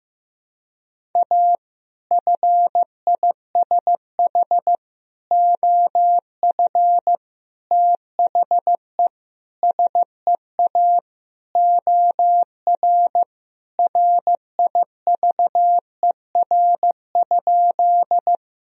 Morse code, heard as a keyed tone, about 15 wpm, 700 Hz. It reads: A FISH OF THE SEA OR RIVER?
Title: Sentences from Top 1000 Words - Review Part 1 - 15wpm